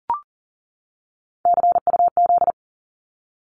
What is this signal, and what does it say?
Morse code, heard as a keyed tone, about 42 wpm, 700 Hz. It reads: CUZ
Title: Abbreviations 40wpm